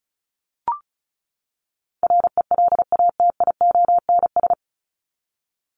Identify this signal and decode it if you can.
Morse code, heard as a keyed tone, about 34 wpm, 700 Hz. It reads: RELATIONS
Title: ES:S Suffix - Top 100 Words with Base Words 35wpm